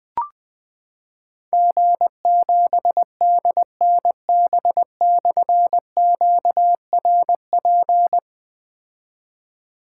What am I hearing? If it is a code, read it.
Morse code, heard as a keyed tone, about 20 wpm, 700 Hz. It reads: G7DNB/QRP
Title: Difficult Call Signs 20wpm